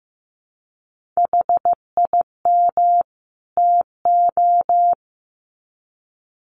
Morse code, heard as a keyed tone, about 15 wpm, 700 Hz. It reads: HIM TO
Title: Sets of 2 Words 15wpm